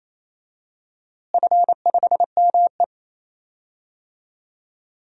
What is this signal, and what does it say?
Morse code, heard as a keyed tone, about 28 wpm, 700 Hz. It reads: F5ME
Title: DXCC-100 Call Signs - 4 Characters - Repeated 3x - 28wpm